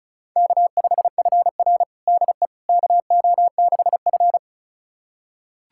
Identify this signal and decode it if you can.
Morse code, heard as a keyed tone, about 36 wpm, 700 Hz. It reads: K5FR DE KO6F
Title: USA Callsign DE Callsign - 4 Characters - 35wpm